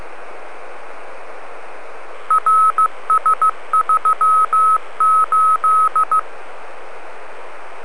Morse code, heard as a keyed tone, about 15 wpm, 1280 Hz. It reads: RS38